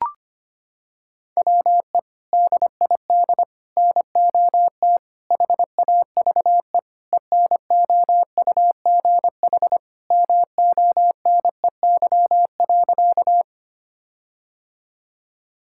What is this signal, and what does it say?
Morse code, heard as a keyed tone, about 25 wpm, 700 Hz. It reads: WE DID NOT HAVE ENOUGH MONEY.
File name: practice / 25wpm / Sentences with 4-Word Phrases - 25wpm